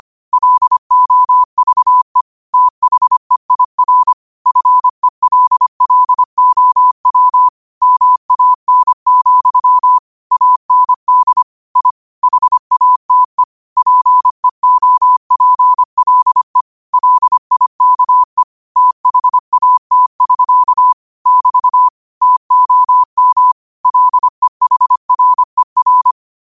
Morse code, heard as a keyed tone, about 25 wpm, 1000 Hz. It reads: LOVE THEIR FELLOW MAN, AND I HATE PEOPLE LIKE THAT<SK> <BT> TOM LEHRER
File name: qotdfiles / 25wpm